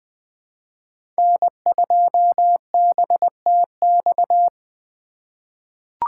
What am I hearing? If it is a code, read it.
Morse code, heard as a keyed tone, about 20 wpm, 700 Hz. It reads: N2BTX
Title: K1USN SST QSO Practice - 20wpm